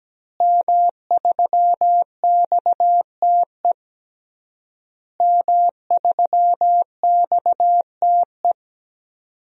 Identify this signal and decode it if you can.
Morse code, heard as a keyed tone, about 17 wpm, 700 Hz. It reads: M3XTE M3XTE
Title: DXCC-100 Call Signs - 5 Characters - Repeated 3x - 17wpm